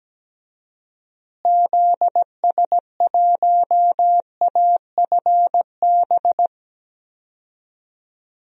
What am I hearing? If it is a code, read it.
Morse code, heard as a keyed tone, about 17 wpm, 700 Hz. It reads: ZS1AFB
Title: DXCC-100 Call Signs - 6 Characters - Repeated 3x - 17wpm